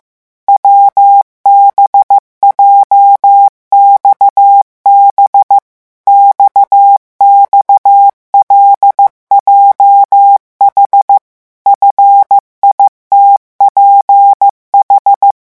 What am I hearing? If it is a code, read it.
Morse code, heard as a keyed tone, about 15 wpm, 800 Hz. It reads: WBJXB XXLJH FITPH
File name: morse mp3 / revision7-vitesse 14 mots